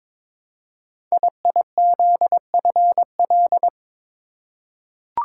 Morse code, heard as a keyed tone, about 22 wpm, 700 Hz. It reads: IIZFL
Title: Code Groups - 5 Letters - 22wpm